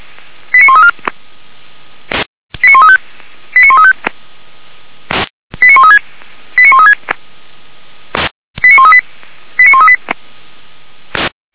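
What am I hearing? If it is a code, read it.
Five-tone selective call (ZVEI), heard as quick runs of five tones, tones of about 80 ms. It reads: 89134 89135 89135 89136 89136 89138 89138
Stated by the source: ZVEI five-tone selective call received by SDR, logged by Feuerwehr Orlamünde fire brigade.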